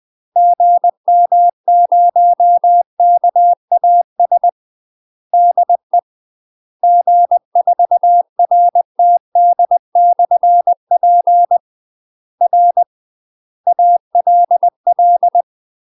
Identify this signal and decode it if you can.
Morse code, heard as a keyed tone, about 20 wpm, 700 Hz. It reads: GM0KAS DE G4RTD/P R ALL
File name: UK MASTER QSOs TRAINING / 20WPM / UK MASTER QSOs 0012